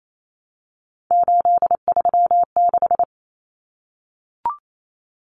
Morse code, heard as a keyed tone, about 28 wpm, 700 Hz. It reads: 836
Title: Numbers - 3-Digit 28wpm